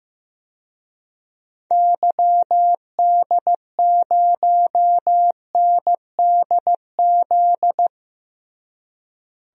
Morse code, heard as a keyed tone, about 15 wpm, 700 Hz. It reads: YD0NDZ
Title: DXCC-100 Call Signs - 6 Characters - Repeated 3x - 15wpm